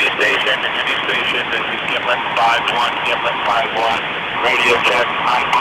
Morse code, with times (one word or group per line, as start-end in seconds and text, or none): none